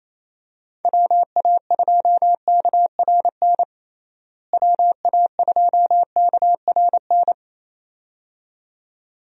Morse code, heard as a keyed tone, about 28 wpm, 700 Hz. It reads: WA2KRN WA2KRN
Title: USA Call Signs - 6 Characters - Repeated 2x - 28wpm